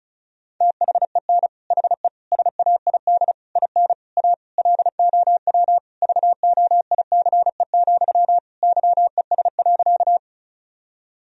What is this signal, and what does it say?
Morse code, heard as a keyed tone, about 36 wpm, 700 Hz. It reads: THEN HE SAID IN A LOW VOICE, YES.
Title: Sentences from Top 1000 Words - Review Part 1 - 35wpm